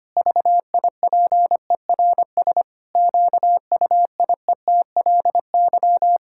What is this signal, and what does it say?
Morse code, heard as a keyed tone, about 25 wpm, 700 Hz. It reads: VIPERS QUIETLY
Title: Pangram Challenge v2 - 25wpm